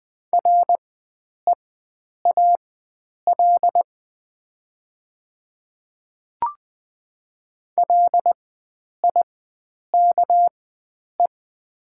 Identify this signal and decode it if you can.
Morse code, heard as a keyed tone, about 20 wpm, spaced as for 9 wpm, 700 Hz. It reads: REAL LIKE
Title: 4x Character Spacing - Top 300 Words - 20wpm